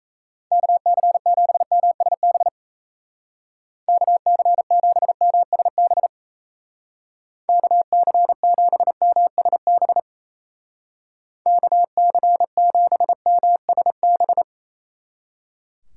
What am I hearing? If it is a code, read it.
Morse code, heard as a keyed tone, about 35 wpm, 700 Hz. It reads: KC7MSB KC7MSB KC7MSB KC7MSB